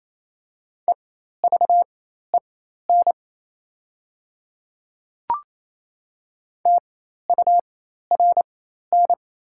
Morse code, heard as a keyed tone, about 28 wpm, spaced as for 13 wpm, 690 Hz. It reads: EVEN TURN
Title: 4x Character Spacing - Top 200 Words - 28wpm